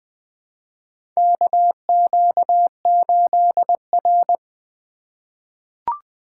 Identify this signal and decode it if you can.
Morse code, heard as a keyed tone, about 20 wpm, 700 Hz. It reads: KQ8R
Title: USA Call Signs - 4 Characters - 2x Character Spacing - 20wpm